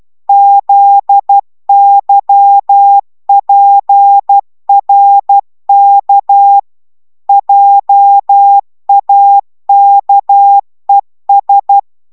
Morse code, heard as a keyed tone, about 12 wpm, 800 Hz. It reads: ZYPRK JAKES